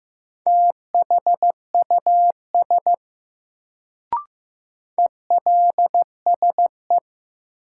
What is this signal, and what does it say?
Morse code, heard as a keyed tone, about 15 wpm, 700 Hz. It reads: THUS ELSE
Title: Sentences from Top 800 Words - New Word Drill - 15wpm